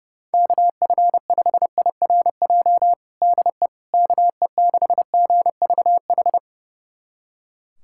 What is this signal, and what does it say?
Morse code, heard as a keyed tone, about 32 wpm, 700 Hz. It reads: KF5IRJ DE KE6GVH
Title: USA Callsign DE Callsign - 6 Characters - 30wpm